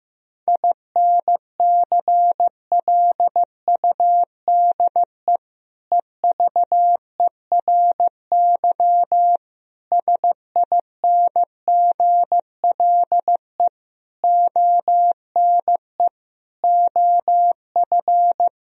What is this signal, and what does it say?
Morse code, heard as a keyed tone, about 15 wpm, 700 Hz. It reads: INCLUDE EVERY SINGLE ONE OF
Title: Sentences from Top 1000 Words - Review Part 4 - 15wpm